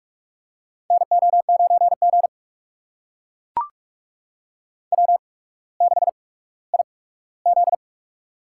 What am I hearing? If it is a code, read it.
Morse code, heard as a keyed tone, about 42 wpm, spaced as for 19 wpm, 700 Hz. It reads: NO9G W6IZ
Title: USA Call Signs - 8x Character Spacing - 45wpm